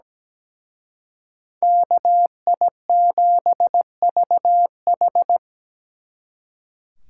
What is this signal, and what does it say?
Morse code, heard as a keyed tone, about 20 wpm, 700 Hz. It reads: KI7VH
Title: Speed Racing - USA 2x2 Call Signs 17wpm